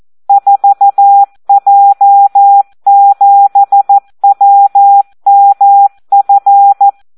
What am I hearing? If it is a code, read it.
Morse code, heard as a keyed tone, about 14 wpm, 800 Hz. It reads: 4J7WMF
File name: Audio practica CW / SD 1 14